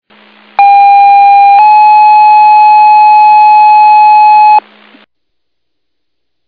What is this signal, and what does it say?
Two-tone page, heard as a one-second tone, then a three-second tone, about 800 then 850 Hz. It is heard from 0.6 to 4.6 s.